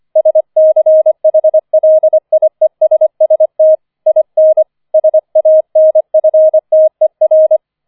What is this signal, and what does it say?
Morse code, heard as a keyed tone, about 25 wpm, 630 Hz. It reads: SCHLIESST IN SANFTER
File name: 004 Vierter Streich 25wpm